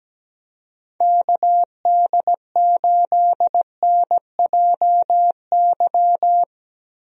Morse code, heard as a keyed tone, about 17 wpm, 700 Hz. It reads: KD8NJY
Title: USA Call Signs - 6 Characters - Repeated 2x - 17wpm